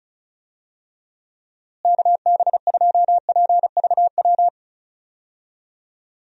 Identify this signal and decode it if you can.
Morse code, heard as a keyed tone, about 34 wpm, 700 Hz. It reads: KB2PVW
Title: USA Call Signs - 6 Characters - Repeated 2x - 35wpm